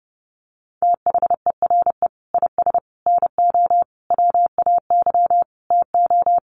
Morse code, heard as a keyed tone, about 30 wpm, 700 Hz. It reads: THERE IS NO WAY TO
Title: Sets of 5 Words Encore 30wpm